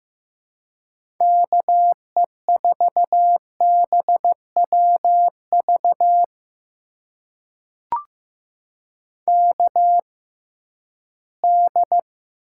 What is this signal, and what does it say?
Morse code, heard as a keyed tone, about 15 wpm, spaced as for 12 wpm, 700 Hz. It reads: KE4BWV KD
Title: USA Call Signs - 6 Characters - 6x Character Spacing - 15wpm